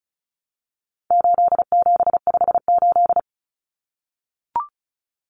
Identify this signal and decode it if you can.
Morse code, heard as a keyed tone, about 34 wpm, 700 Hz. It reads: 8758
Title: Numbers - 4-Digit 35wpm